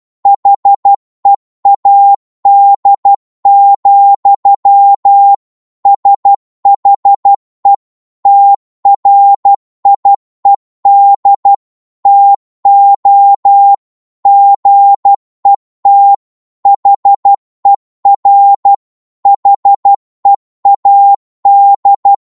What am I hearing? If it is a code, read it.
Morse code, heard as a keyed tone, about 12 wpm, 800 Hz. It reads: HEAD, SHE TRIED TO GET HER HEAD